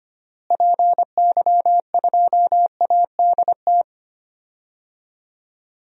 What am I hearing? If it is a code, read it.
Morse code, heard as a keyed tone, about 25 wpm, 700 Hz. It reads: PY2ADT
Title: DXCC-100 Call Signs - 6 Characters - Repeated 2x - 25wpm